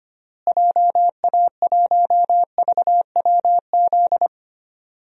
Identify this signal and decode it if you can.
Morse code, heard as a keyed tone, about 25 wpm, 700 Hz. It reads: JA1VWZ
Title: DXCC-100 Call Signs - 6 Characters - Repeated 3x - 25wpm